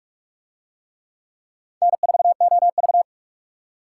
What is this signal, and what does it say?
Morse code, heard as a keyed tone, about 41 wpm, 700 Hz. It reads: N4OV